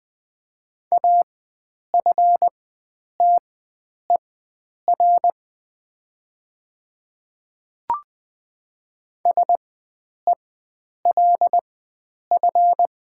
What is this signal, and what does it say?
Morse code, heard as a keyed tone, about 20 wpm, spaced as for 9 wpm, 700 Hz. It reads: AFTER SELF
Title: 4x Character Spacing - Top 500 Words - 20wpm